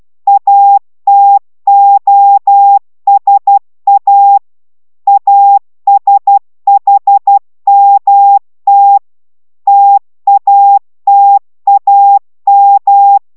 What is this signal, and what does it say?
Morse code, heard as a keyed tone, about 12 wpm, 800 Hz. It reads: ATOSA ASHMT TATAM
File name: taviro tanulas / cw 69 04 10